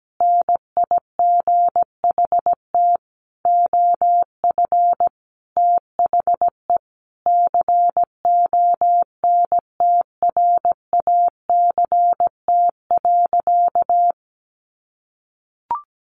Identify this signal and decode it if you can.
Morse code, heard as a keyed tone, about 17 wpm, 700 Hz. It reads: NIGHT OF THE CONTRACT.
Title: Sentences from Top 1400 Words 17wpm